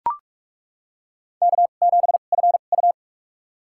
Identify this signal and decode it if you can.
Morse code, heard as a keyed tone, about 42 wpm, spaced as for 31 wpm, 700 Hz. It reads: K7FU
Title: USA Call Signs - 2x Character Spacing - 45wpm